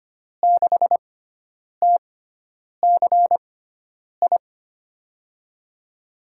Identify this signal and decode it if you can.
Morse code, heard as a keyed tone, about 25 wpm, spaced as for 9 wpm, 700 Hz. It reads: 6TCI
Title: USA Call Signs - 6x Character Spacing - 25wpm